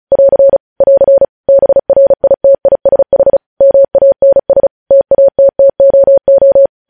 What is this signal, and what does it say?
Morse code, heard as a keyed tone, about 36 wpm, 550 Hz. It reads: <AR> <AR> BRITISH MANS TATTOO
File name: podcast / News headlines-1x-35x35